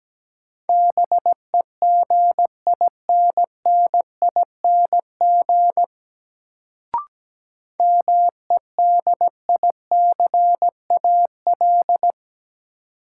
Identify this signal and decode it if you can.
Morse code, heard as a keyed tone, about 17 wpm, 700 Hz. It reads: BEGINNING MEDICAL
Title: Sentences from Top 800 Words - New Word Drill - 17wpm